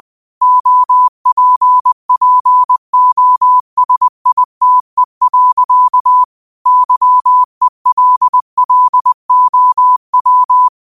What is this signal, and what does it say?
Morse code, heard as a keyed tone, about 20 wpm, 1000 Hz. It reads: OPPOSITE. YELLOW
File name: qotdfiles / 20wpm